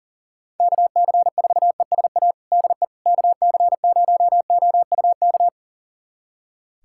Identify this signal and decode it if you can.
Morse code, heard as a keyed tone, about 40 wpm, 690 Hz. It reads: KC4ESA DE KC0OUK
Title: USA Callsign DE Callsign - 6 Characters - 40wpm